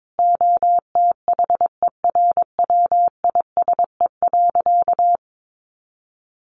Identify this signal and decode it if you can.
Morse code, heard as a keyed tone, about 22 wpm, 700 Hz. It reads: OTHERWISE.
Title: Sentences from Top 1500 Words 22wpm